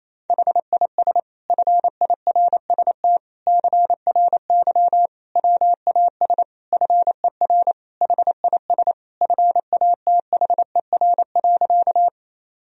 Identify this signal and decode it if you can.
Morse code, heard as a keyed tone, about 28 wpm, 700 Hz. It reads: HIS FIRST CRY WAS FER HIS FATHER.
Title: Sentences from Top 500 Words and Abbreviations 28wpm